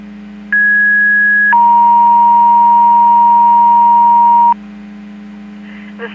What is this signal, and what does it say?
Two-tone page, heard as a one-second tone, then a three-second tone, about 1650 then 940 Hz. It is heard from 0.5 to 4.5 s.
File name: Portage 154.385 Paging